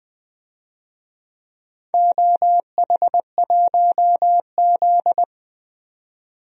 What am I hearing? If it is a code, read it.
Morse code, heard as a keyed tone, about 20 wpm, 700 Hz. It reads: OH1Z
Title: DXCC-100 Call Signs - 4 Characters - 20wpm